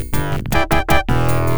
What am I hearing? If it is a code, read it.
There is no code.